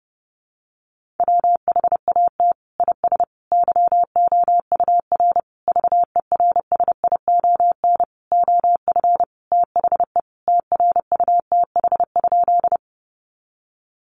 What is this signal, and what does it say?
Morse code, heard as a keyed tone, about 30 wpm, 700 Hz. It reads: WHAT IS YOUR VERSION OF THE TRUTH?
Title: Sentences from Top 1400 Words 30wpm